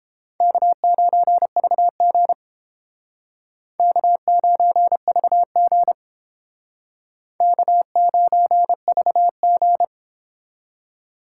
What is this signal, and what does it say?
Morse code, heard as a keyed tone, about 31 wpm, 700 Hz. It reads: K9VG K9VG K9VG